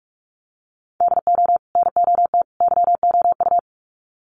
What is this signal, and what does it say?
Morse code, heard as a keyed tone, about 44 wpm, 700 Hz. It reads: DO NOT YOU
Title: Sets of 3 Words 45wpm